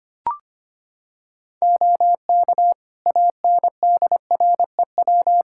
Morse code, heard as a keyed tone, about 25 wpm, 700 Hz. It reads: OK ANDREW
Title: Practice QSO - Exchange 3 - More Info - 25wpm